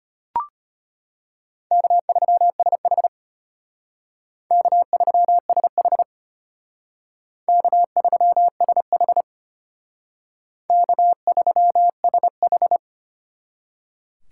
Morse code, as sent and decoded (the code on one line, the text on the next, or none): -.- ...-- ... .... / -.- ...-- ... .... / -.- ...-- ... .... / -.- ...-- ... ....
K3SH K3SH K3SH K3SH